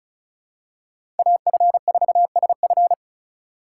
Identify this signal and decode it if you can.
Morse code, heard as a keyed tone, about 36 wpm, 700 Hz. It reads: AF4SF
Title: USA Call Signs - 2x Character Spacing - 35wpm